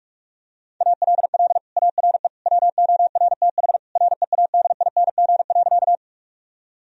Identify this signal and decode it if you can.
Morse code, heard as a keyed tone, about 42 wpm, 700 Hz. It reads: ALL ARE WORTH READING.
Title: Sentences from Top 1000 Words - Review Part 3 - 45wpm